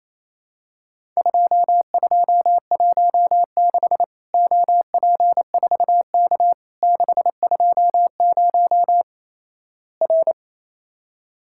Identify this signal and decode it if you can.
Morse code, heard as a keyed tone, about 28 wpm, 700 Hz. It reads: 2216 OP4K 620 R
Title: Worked All Europe DX Contest - QTC Exchange - 28wpm